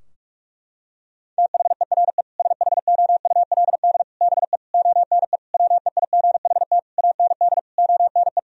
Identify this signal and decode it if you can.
Morse code, heard as a keyed tone, about 41 wpm, 700 Hz. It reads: THERE SHOULD BE ONE WEIGHT AND ONE
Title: Sentences from Top 1000 Words - Review Part 3 - 45wpm